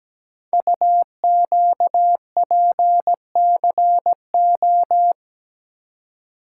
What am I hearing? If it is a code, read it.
Morse code, heard as a keyed tone, about 17 wpm, 700 Hz. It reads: UQPCO
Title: Code Groups - 5 Letters - 17wpm